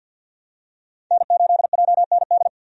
Morse code, heard as a keyed tone, about 52 wpm, 700 Hz. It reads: N8JND
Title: USA Call Signs - 5 Characters - 50wpm